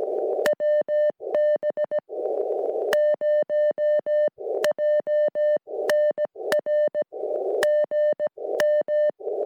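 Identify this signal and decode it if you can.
Morse code, heard as a keyed tone, about 18 wpm, spaced as for 13 wpm, 600 Hz. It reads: WB0JNR GM
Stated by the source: QRP SOTA contact between N5YCO and WB0JNR on Berrian Mountain, 20 meters.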